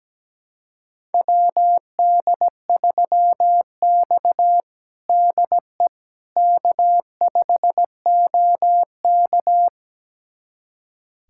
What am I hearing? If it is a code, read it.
Morse code, heard as a keyed tone, about 17 wpm, 700 Hz. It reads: WD3X DE K5OK
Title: USA Callsign DE Callsign - 4 Characters - 17wpm